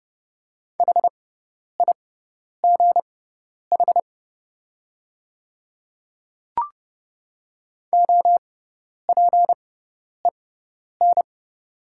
Morse code, heard as a keyed tone, about 30 wpm, spaced as for 10 wpm, 700 Hz. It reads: HIGH OPEN
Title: 6x Character Spacing - Top 300 Words - 30wpm